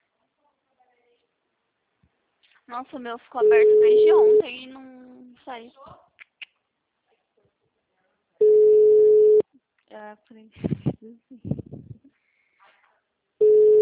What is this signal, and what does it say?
Ringback tone (Europe, 425 Hz), with speech.